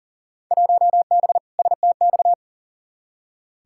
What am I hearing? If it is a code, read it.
Morse code, heard as a keyed tone, about 40 wpm, 700 Hz. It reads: 1B STX
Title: Field Day Exchanges Encore - 40wpm